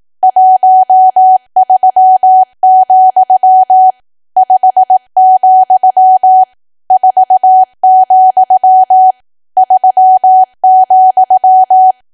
Morse code, heard as a keyed tone, about 18 wpm, 750 Hz. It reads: 13, 5, 4, 3,
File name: cwa / W1AW-Level2 / 18WPM